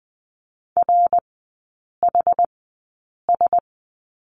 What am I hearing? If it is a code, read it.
Morse code, heard as a keyed tone, about 20 wpm, spaced as for 8 wpm, 690 Hz. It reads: RHS
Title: 060 Lesson 11e - Letter h - Three Characters 20wpm